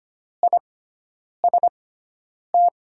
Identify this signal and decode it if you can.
Morse code, heard as a keyed tone, about 25 wpm, spaced as for 9 wpm, 700 Hz. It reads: IST
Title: USA Call Signs - 5 Characters - 6x Character Spacing - 25wpm